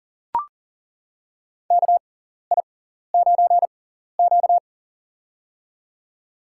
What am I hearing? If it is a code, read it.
Morse code, heard as a keyed tone, about 40 wpm, spaced as for 14 wpm, 700 Hz. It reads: KI9Q
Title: USA Call Signs - 4 Characters - 6x Character Spacing - 40wpm